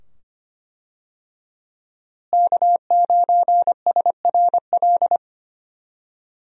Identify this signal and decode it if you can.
Morse code, heard as a keyed tone, about 25 wpm, 700 Hz. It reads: K9SRL
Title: USA Call Signs - 5 Characters - Repeated 2x - 25wpm